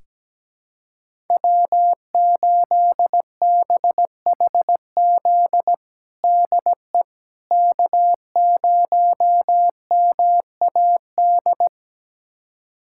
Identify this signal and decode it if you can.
Morse code, heard as a keyed tone, about 17 wpm, 700 Hz. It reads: W8BHZ DE K0MAD
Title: USA Callsign DE Callsign - 5 Characters - 17wpm